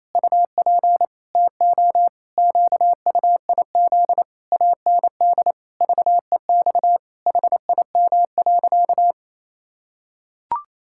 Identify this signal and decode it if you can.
Morse code, heard as a keyed tone, about 28 wpm, 700 Hz. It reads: UP TO QUIZ AND VEX HIM.
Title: Pangram Challenge v2 - 28wpm